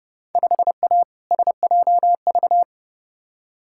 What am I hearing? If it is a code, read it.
Morse code, heard as a keyed tone, about 32 wpm, 700 Hz. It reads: 5A SJV
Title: Field Day Exchanges - 30wpm